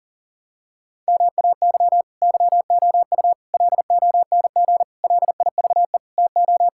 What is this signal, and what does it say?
Morse code, heard as a keyed tone, about 40 wpm, 700 Hz. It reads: MAY YOU LONG LIVE TO
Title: Sentences from Top 1000 Words - Review Part 6 - 40wpm